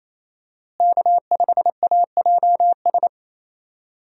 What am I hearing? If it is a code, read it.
Morse code, heard as a keyed tone, about 27 wpm, 700 Hz. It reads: K5AJS
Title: USA Call Signs - 5 Characters - Repeated 3x - 28wpm